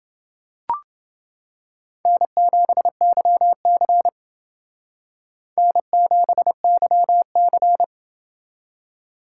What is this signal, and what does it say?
Morse code, heard as a keyed tone, about 29 wpm, 700 Hz. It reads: N7YC N7YC